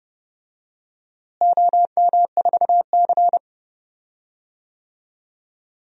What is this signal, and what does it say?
Morse code, heard as a keyed tone, about 30 wpm, 700 Hz. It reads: OM4C
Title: DXCC-100 Call Signs - 4 Characters - Repeated 3x - 30wpm